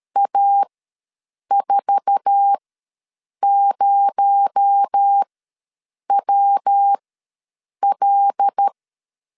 Morse code, heard as a keyed tone, about 13 wpm, spaced as for 7 wpm, 800 Hz. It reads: A40WL
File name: LEZIONI / 8 YJ45 / 8 YJ4 ws 12wpm 3249